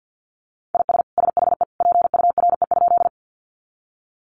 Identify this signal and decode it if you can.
Morse code, heard as a keyed tone, about 50 wpm, 700 Hz. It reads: IS SHE PURE?
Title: Sentences from Top 1500 Words 50wpm